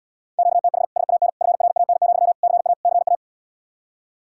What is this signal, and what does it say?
Morse code, heard as a keyed tone, about 80 wpm, 700 Hz. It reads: QTH SAN FRAN, CA CA